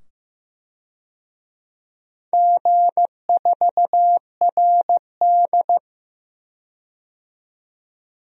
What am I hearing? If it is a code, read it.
Morse code, heard as a keyed tone, about 15 wpm, 700 Hz. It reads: G4RD
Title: DXCC-100 Call Signs - 4 Characters - 15wpm